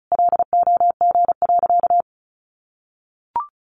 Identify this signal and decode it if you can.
Morse code, heard as a keyed tone, about 34 wpm, 700 Hz. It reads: LOG.
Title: Sentences from Top 1500 Words 35wpm